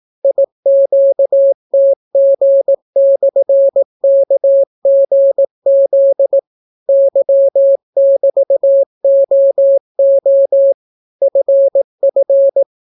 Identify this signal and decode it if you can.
Morse code, heard as a keyed tone, about 18 wpm, 550 Hz. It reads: IQTG/KGZ Y<BT>OO FF